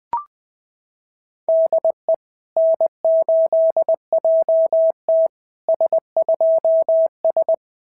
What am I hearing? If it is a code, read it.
Morse code, heard as a keyed tone, about 21 wpm, 690 Hz. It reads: DE N8JT S2S
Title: SOTA QSO Practice 20wpm